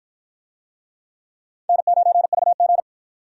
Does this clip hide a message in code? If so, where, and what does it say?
Yes: Morse code from 1.7 to 2.8 s, heard as a keyed tone, about 53 wpm, 700 Hz. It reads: N9VG